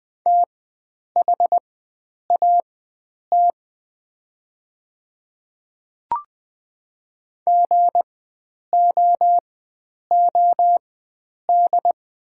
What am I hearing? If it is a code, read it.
Morse code, heard as a keyed tone, about 20 wpm, spaced as for 9 wpm, 700 Hz. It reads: THAT GOOD
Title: 4x Character Spacing - Top 300 Words - 20wpm